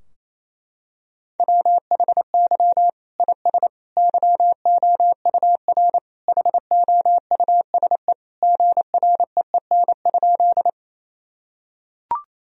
Morse code, heard as a keyed tone, about 28 wpm, 700 Hz. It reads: WHY IS YOUR HOUSE GREEN?
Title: Sentences from Top 1000 Words - Review Part 2 - 28wpm